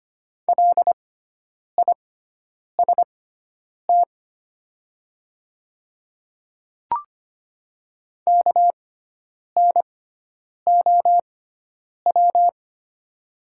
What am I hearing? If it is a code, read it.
Morse code, heard as a keyed tone, about 25 wpm, spaced as for 9 wpm, 700 Hz. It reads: LIST KNOW
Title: 6x Character Spacing - Top 500 Words - 25wpm